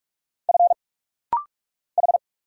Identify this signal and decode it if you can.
Morse code, heard as a keyed tone, about 44 wpm, 700 Hz. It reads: F H
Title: Single Letter-Number - Rapid-Fire 45wpm